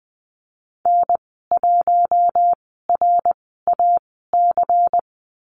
Morse code, heard as a keyed tone, about 20 wpm, spaced as for 15 wpm, 700 Hz. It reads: N1RAC
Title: 122 Lesson 20g - Number 3 - Call Signs 20wpm